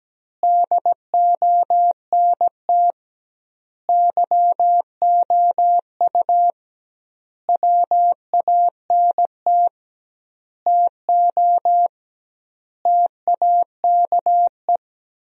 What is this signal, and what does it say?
Morse code, heard as a keyed tone, about 17 wpm, 700 Hz. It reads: DONT YOU WANT TO TAKE
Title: 2x Word-Spacing - Sentences from Top 300 Words - 17wpm